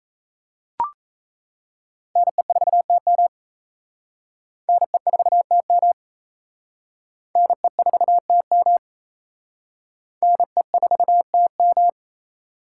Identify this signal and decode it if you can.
Morse code, heard as a keyed tone, about 33 wpm, 700 Hz. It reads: NE4TM NE4TM NE4TM NE4TM
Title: Speed Racing - USA 2x2 Call Signs 28wpm